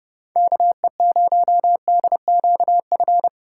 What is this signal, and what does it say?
Morse code, heard as a keyed tone, about 30 wpm, 700 Hz. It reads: KE0DQF
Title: USA Call Signs - 6 Characters - 2x Character Spacing - 30wpm